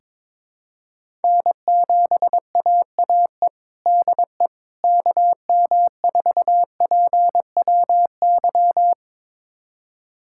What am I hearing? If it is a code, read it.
Morse code, heard as a keyed tone, about 21 wpm, 700 Hz. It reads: N7AAE DE KM4PWY
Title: USA Callsign DE Callsign 22wpm